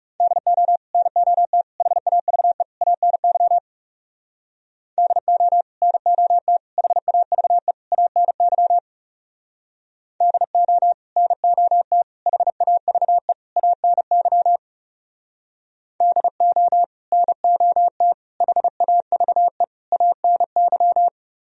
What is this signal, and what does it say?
Morse code, heard as a keyed tone, about 40 wpm, 700 Hz. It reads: DO NOT HAVE ANY DO NOT HAVE ANY DO NOT HAVE ANY DO NOT HAVE ANY